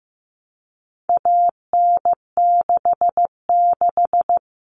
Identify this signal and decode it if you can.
Morse code, heard as a keyed tone, about 15 wpm, 700 Hz. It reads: AN66
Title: All Cut Numbers - 4-Digit 15wpm